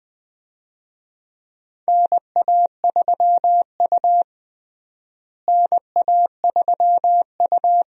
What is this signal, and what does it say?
Morse code, heard as a keyed tone, about 21 wpm, 700 Hz. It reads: NA3U NA3U
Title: USA Call Signs - 4 Characters - Repeated 3x - 20wpm